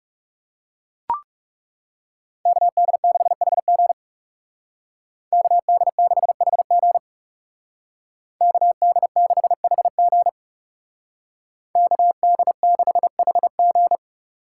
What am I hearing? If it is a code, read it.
Morse code, heard as a keyed tone, about 39 wpm, 700 Hz. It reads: KD6HG KD6HG KD6HG KD6HG